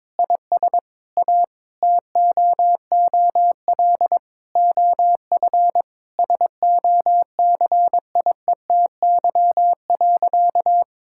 Morse code, heard as a keyed tone, about 22 wpm, 700 Hz. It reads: IS A TOOL OF SOCIETY.
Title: Sentences from Top 1000 Words - Review Part 6 - 22wpm